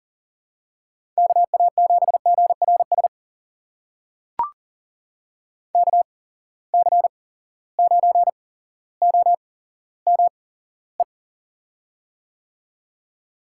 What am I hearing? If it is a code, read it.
Morse code, heard as a keyed tone, about 40 wpm, spaced as for 17 wpm, 700 Hz. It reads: KA7GRS KC9OME
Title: USA Call Signs - 6 Characters - 8x Character Spacing - 40wpm